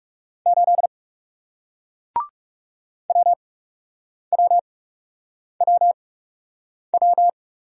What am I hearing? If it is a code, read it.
Morse code, heard as a keyed tone, about 40 wpm, spaced as for 8 wpm, 700 Hz. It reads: 8 WWWW